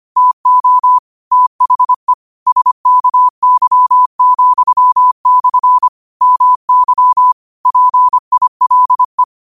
Morse code, heard as a keyed tone, about 25 wpm, 1000 Hz. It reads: TO THE SKY,/ MY PILE
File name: qotdfiles / 25wpm